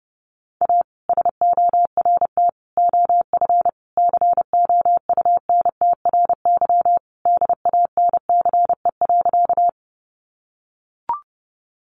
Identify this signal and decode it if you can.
Morse code, heard as a keyed tone, about 30 wpm, 700 Hz. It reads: A SORT OF COUNTRY DANCE.
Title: Sentences from Top 1300 Words 30wpm